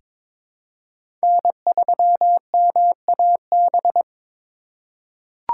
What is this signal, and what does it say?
Morse code, heard as a keyed tone, about 22 wpm, 700 Hz. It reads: N3MAB
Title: USA Call Signs - 4x Character Spacing - 22wpm